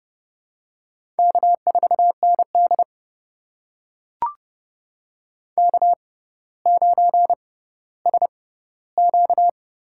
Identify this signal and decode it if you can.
Morse code, heard as a keyed tone, about 30 wpm, spaced as for 15 wpm, 700 Hz. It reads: K4ND K9SQ
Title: USA Call Signs - 4 Characters - 6x Character Spacing - 30wpm